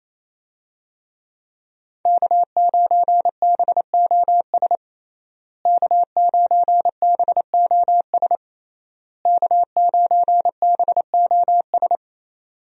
Morse code, heard as a keyed tone, about 28 wpm, 700 Hz. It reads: K9BOS K9BOS K9BOS
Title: USA Call Signs - 5 Characters - Repeated 3x - 28wpm